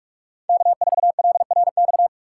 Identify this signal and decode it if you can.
Morse code, heard as a keyed tone, about 41 wpm, 700 Hz. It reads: K4LRX
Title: K1USN SST QSO Practice - 45wpm